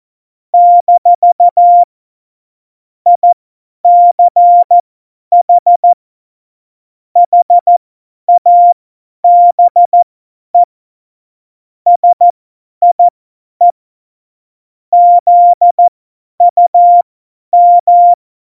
Morse code, heard as a keyed tone, about 14 wpm, spaced as for 10 wpm, 700 Hz. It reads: - ICH HABE SIE ZUM
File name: Morsekurs / Witze / Witz 22. Dezember 2020, 50 F70